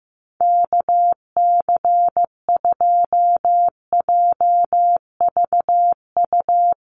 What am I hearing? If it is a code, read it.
Morse code, heard as a keyed tone, about 15 wpm, 700 Hz. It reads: KC2JVU
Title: USA Call Signs 15wpm